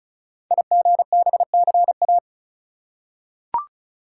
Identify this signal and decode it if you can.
Morse code, heard as a keyed tone, about 36 wpm, 700 Hz. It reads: IGBCA
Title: Code Groups - 5 Letters - 35wpm